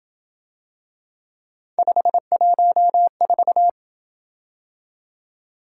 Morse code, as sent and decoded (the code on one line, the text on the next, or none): ..... .---- ....-
514